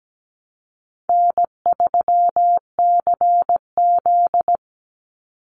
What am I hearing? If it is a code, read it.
Morse code, heard as a keyed tone, about 17 wpm, 700 Hz. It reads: N3CZ
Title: Call Signs Repeated 2x - 17wpm